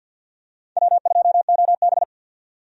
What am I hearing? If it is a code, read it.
Morse code, heard as a keyed tone, about 52 wpm, 700 Hz. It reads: W2OB
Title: USA Call Signs - 4 Characters - 50wpm